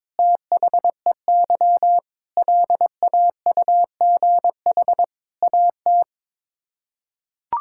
Morse code, heard as a keyed tone, about 22 wpm, 700 Hz. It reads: THEY LAUGH AT
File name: practice / 22wpm / Sets of 8 Words - 22wpm